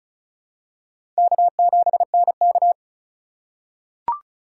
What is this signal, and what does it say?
Morse code, heard as a keyed tone, about 36 wpm, 700 Hz. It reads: K7NK
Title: USA Call Signs - 4 Characters - 4x Character Spacing - 35wpm